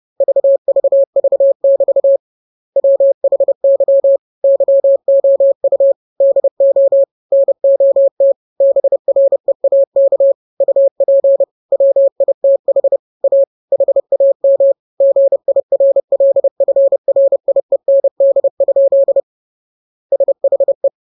Morse code, heard as a keyed tone, about 30 wpm, 550 Hz. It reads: VVV<BT> WHY YOU DO NOT BREAK UP WITH A HAM GIRLFRIEND? SHE